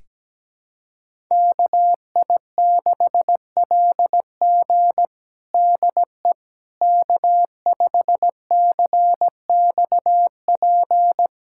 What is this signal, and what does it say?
Morse code, heard as a keyed tone, about 17 wpm, 700 Hz. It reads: KI6LG DE K5CXP
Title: USA Callsign DE Callsign - 5 Characters - 17wpm